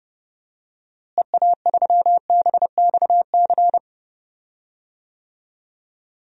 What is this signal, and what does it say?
Morse code, heard as a keyed tone, about 29 wpm, 700 Hz. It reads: EA3BXC
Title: DXCC-100 Call Signs - 6 Characters - Repeated 3x - 30wpm